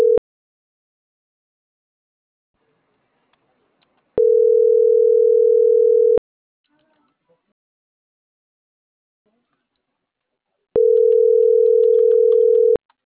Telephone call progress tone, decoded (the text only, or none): ringback tone